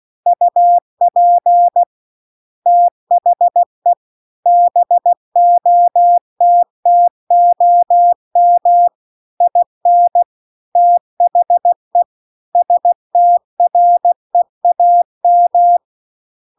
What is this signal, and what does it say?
Morse code, heard as a keyed tone, about 16 wpm, 700 Hz. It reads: UP THE BOTTOM IN THE STREAM
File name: CW-The-Big-Twoheaded-River-2-16WPM